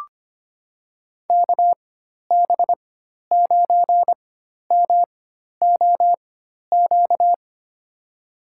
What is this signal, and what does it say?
Morse code, heard as a keyed tone, about 25 wpm, spaced as for 12 wpm, 700 Hz. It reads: KB9MOQ
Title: USA Call Signs - 6 Characters - 4x Character Spacing - 25wpm